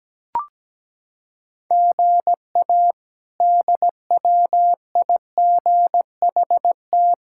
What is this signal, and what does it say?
Morse code, heard as a keyed tone, about 17 wpm, 700 Hz. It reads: GA DWIGHT
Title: K1USN SST QSO Practice - 17wpm